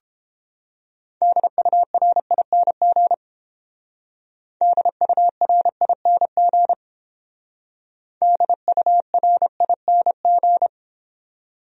Morse code, heard as a keyed tone, about 31 wpm, 700 Hz. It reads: DURING DURING DURING